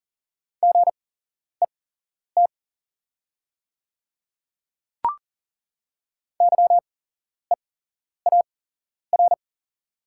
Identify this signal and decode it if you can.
Morse code, heard as a keyed tone, about 41 wpm, spaced as for 11 wpm, 690 Hz. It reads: GET YEAR
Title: 8x Character Spacing - Top 200 Words - 40wpm